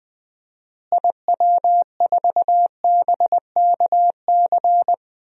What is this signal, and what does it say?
Morse code, heard as a keyed tone, about 21 wpm, 700 Hz. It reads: IW4BKC
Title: DXCC-100 Call Signs - 6 Characters - Repeated 2x - 20wpm